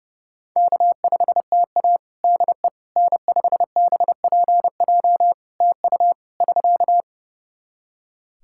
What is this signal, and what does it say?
Morse code, heard as a keyed tone, about 32 wpm, 700 Hz. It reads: K5TA DE N5BPJ TU <SK>
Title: K1USN SST QSO Practice 30wpm